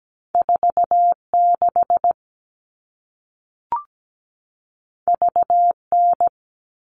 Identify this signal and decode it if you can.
Morse code, heard as a keyed tone, about 17 wpm, 700 Hz. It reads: 46 VN
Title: All Cut Numbers - 2-Digit 17wpm